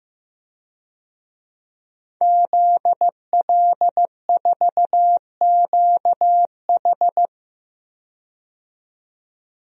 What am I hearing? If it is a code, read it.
Morse code, heard as a keyed tone, about 15 wpm, 700 Hz. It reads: ZL4QH
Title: DXCC-100 Call Signs - 5 Characters - 15wpm